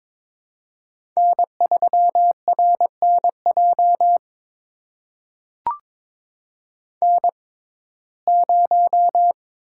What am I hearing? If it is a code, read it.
Morse code, heard as a keyed tone, about 22 wpm, spaced as for 16 wpm, 700 Hz. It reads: N3RNJ N0
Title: USA Call Signs - 5 Characters - 6x Character Spacing - 22wpm